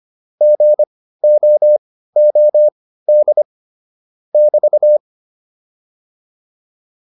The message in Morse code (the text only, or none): GOOD <BT>